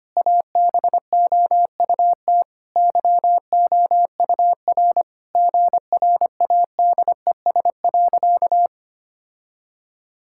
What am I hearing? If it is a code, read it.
Morse code, heard as a keyed tone, about 25 wpm, 700 Hz. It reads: ABOUT YOUR GRADES.
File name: practice / 25wpm / Sentences with 3-Word Phrases - 25wpm